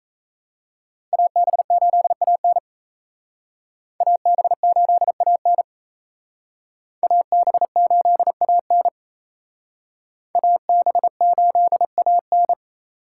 Morse code, heard as a keyed tone, about 34 wpm, 700 Hz. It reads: AB8AN AB8AN AB8AN AB8AN
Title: Speed Racing - USA 2x2 Call Signs 28wpm